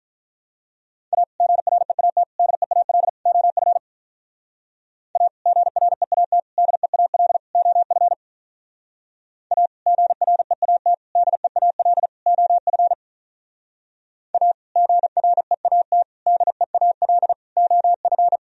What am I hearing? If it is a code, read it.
Morse code, heard as a keyed tone, about 44 wpm, 700 Hz. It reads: A GREAT DEAL OF A GREAT DEAL OF A GREAT DEAL OF A GREAT DEAL OF